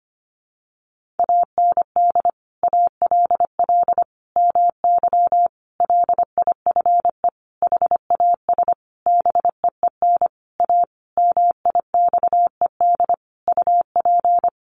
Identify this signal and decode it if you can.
Morse code, heard as a keyed tone, about 25 wpm, 700 Hz. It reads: AND ALL MY LIFE HAS BEEN A MIXED UP
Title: Sentences from Top 1500 Words 25wpm